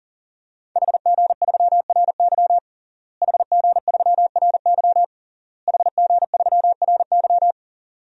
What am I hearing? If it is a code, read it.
Morse code, heard as a keyed tone, about 40 wpm, 700 Hz. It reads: HG3RY HG3RY HG3RY
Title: DXCC-100 Call Signs - 5 Characters - Repeated 3x - 40wpm